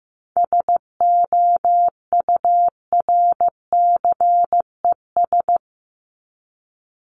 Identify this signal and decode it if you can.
Morse code, heard as a keyed tone, about 15 wpm, 700 Hz. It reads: SOURCES
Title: ES:S Suffix - Top 100 Words with Base Words 15wpm